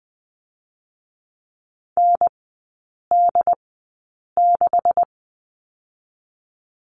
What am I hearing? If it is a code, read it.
Morse code, heard as a keyed tone, about 20 wpm, 700 Hz. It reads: N D 6
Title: 145 Lesson 24e - Letter y - Three Characters 20wpm